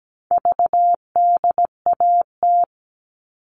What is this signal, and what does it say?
Morse code, heard as a keyed tone, about 17 wpm, 700 Hz. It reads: VDAT
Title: All Cut Numbers - 4-Digit 17wpm